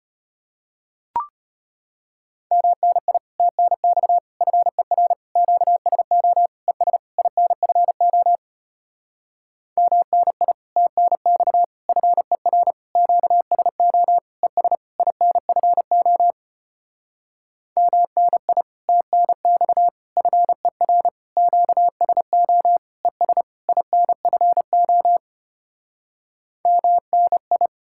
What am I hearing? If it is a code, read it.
Morse code, heard as a keyed tone, about 33 wpm, 700 Hz. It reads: MNI TNX FER QSO ES INFO MNI TNX FER QSO ES INFO MNI TNX FER QSO ES INFO MNI